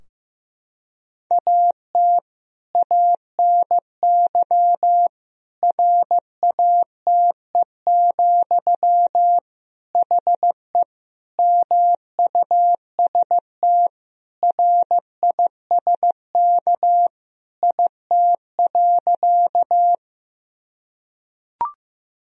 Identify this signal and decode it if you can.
Morse code, heard as a keyed tone, about 15 wpm, 700 Hz. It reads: AT ANY RATE, HE MUST RISK IT.
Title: Sentences from Top 1000 Words - Review Part 4 - 15wpm